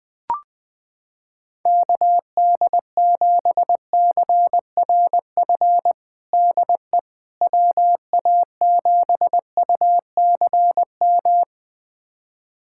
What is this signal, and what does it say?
Morse code, heard as a keyed tone, about 20 wpm, 700 Hz. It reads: KD7CRF DE WA7UCM
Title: USA Callsign DE Callsign - 6 Characters - 20wpm